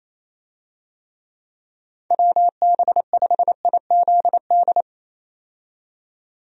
Morse code, heard as a keyed tone, about 28 wpm, 700 Hz. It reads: WB5IZD
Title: USA Call Signs - 6 Characters - 28wpm